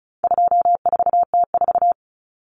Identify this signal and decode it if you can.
Morse code, heard as a keyed tone, about 34 wpm, 700 Hz. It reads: 24T4
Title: Common Cut Numbers - 4-Digit 35wpm